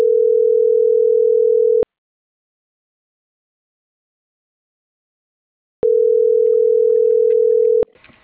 Ringback tone (North America), 440 + 480 Hz.